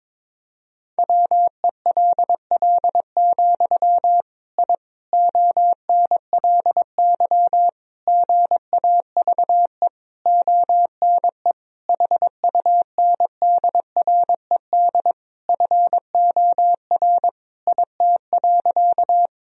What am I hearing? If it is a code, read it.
Morse code, heard as a keyed tone, about 22 wpm, 700 Hz. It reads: WELL, I ONLY GAVE ONE HUNDRED FOR IT.
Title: Sentences from Top 1000 Words - Review Part 2 - 22wpm